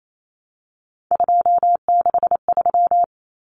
Morse code, heard as a keyed tone, about 28 wpm, 700 Hz. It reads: 263
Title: Numbers - 3-Digit 28wpm